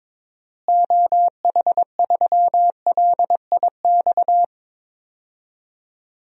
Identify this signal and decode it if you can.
Morse code, heard as a keyed tone, about 22 wpm, 700 Hz. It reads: OH3LIX
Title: DXCC-100 Call Signs - 6 Characters - Repeated 2x - 22wpm